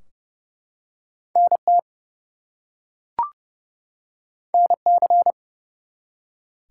Morse code, heard as a keyed tone, about 32 wpm, 700 Hz. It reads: NT NC
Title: US States and Canadian Provinces 30wpm